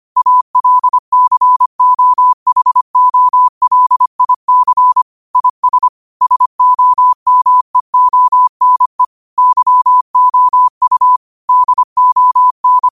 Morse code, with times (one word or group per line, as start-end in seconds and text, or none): 0.2-5.0 ALCOHOLIC
5.4-5.9 IS
6.2-9.0 SOMEONE
9.4-11.2 YOU
11.5-12.9 DON